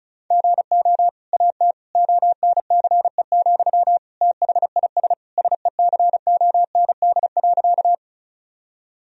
Morse code, heard as a keyed tone, about 36 wpm, 700 Hz. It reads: GO AT ONCE, THIS SECOND.
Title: Sentences from Top 1000 Words - Review Part 1 - 35wpm